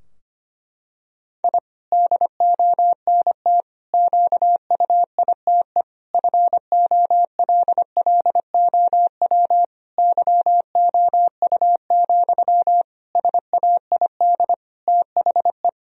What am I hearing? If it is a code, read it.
Morse code, heard as a keyed tone, about 25 wpm, 700 Hz. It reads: I DONT QUITE FOLLOW YOU, SAID THE
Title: Sentences from Top 1000 Words - Review Part 4 - 25wpm